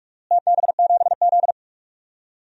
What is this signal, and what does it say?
Morse code, heard as a keyed tone, about 46 wpm, 700 Hz. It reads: TB7Z
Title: DXCC-100 Call Signs - 4 Characters - Repeated 2x - 45wpm